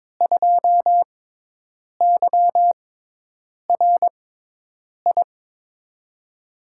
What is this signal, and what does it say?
Morse code, heard as a keyed tone, about 22 wpm, spaced as for 8 wpm, 700 Hz. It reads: 2YRI
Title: USA Call Signs - 6 Characters - 6x Character Spacing - 22wpm